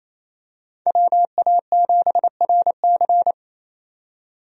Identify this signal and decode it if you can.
Morse code, heard as a keyed tone, about 28 wpm, 700 Hz. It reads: WA7RC
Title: USA Call Signs - 5 Characters - 28wpm